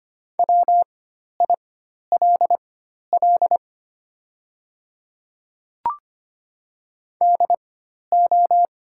Morse code, heard as a keyed tone, about 25 wpm, spaced as for 12 wpm, 690 Hz. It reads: WILL DO
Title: 4x Character Spacing - Top 100 Words - 25wpm